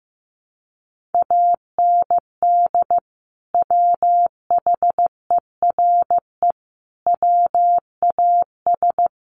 Morse code, heard as a keyed tone, about 15 wpm, 700 Hz. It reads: AND WHERE WAS
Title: Sentences from Top 200 Words 15wpm